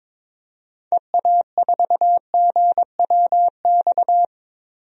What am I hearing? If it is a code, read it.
Morse code, heard as a keyed tone, about 22 wpm, 700 Hz. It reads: EA4GWX
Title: DXCC-100 Call Signs - 6 Characters - Repeated 2x - 22wpm